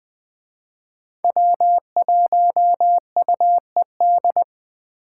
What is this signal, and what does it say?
Morse code, heard as a keyed tone, about 20 wpm, 700 Hz. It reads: W1UED
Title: Speed Racing - USA 1x3 Call Signs 17wpm